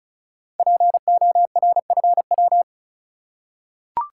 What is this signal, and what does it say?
Morse code, heard as a keyed tone, about 36 wpm, 700 Hz. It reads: PORFW
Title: Code Groups - 5 Letters - 35wpm